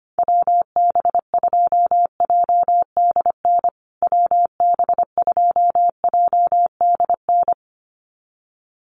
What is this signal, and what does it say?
Morse code, heard as a keyed tone, about 25 wpm, 700 Hz. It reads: WB2JDN WB2JDN
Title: USA Call Signs Repeated 2x 25wpm